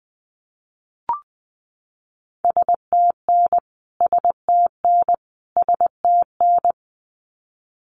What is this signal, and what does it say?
Morse code, heard as a keyed tone, about 20 wpm, 700 Hz. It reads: STN STN STN
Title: Abbreviations Repeated 3x - 20wpm